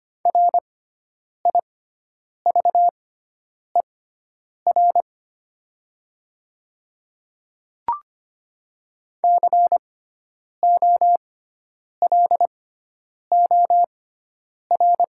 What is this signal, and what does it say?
Morse code, heard as a keyed tone, about 25 wpm, spaced as for 9 wpm, 700 Hz. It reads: RIVER COLOR
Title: 6x Character Spacing - Top 500 Words - 25wpm